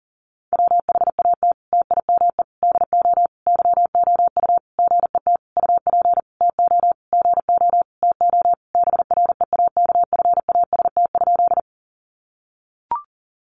Morse code, heard as a keyed tone, about 40 wpm, 700 Hz. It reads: WHAT TIME DO YOU GET UP TO GO TO BREAKFAST?
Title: Sentences from Top 1500 Words 40wpm